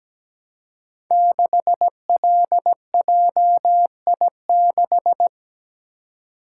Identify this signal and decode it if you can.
Morse code, heard as a keyed tone, about 17 wpm, 700 Hz. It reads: 6LJI6
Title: Code Groups - 5 Letter:Numbers - 17wpm